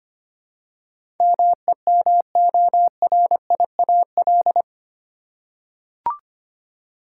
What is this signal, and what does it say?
Morse code, heard as a keyed tone, about 25 wpm, 700 Hz. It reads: MEMORIAL
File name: Eight Letter Words - 25wpm